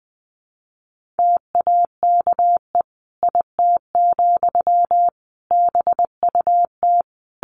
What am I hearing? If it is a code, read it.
Morse code, heard as a keyed tone, about 20 wpm, 700 Hz. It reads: TAKE IT, BUT
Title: Sentences from Top 1500 Words 20wpm